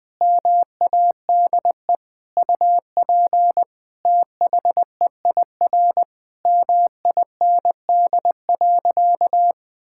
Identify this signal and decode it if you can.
Morse code, heard as a keyed tone, about 21 wpm, 700 Hz. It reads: MADE UP THEIR MIND.
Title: Sentences from Top 1000 Words - Review Part 5 - 20wpm